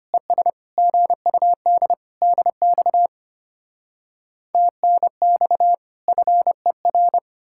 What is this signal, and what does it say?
Morse code, heard as a keyed tone, about 33 wpm, 700 Hz. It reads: ES GUD DX TNX FER